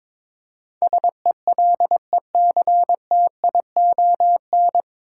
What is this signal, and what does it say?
Morse code, heard as a keyed tone, about 22 wpm, 700 Hz. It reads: SELECTION
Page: Morse Code Ninja practice 22wpm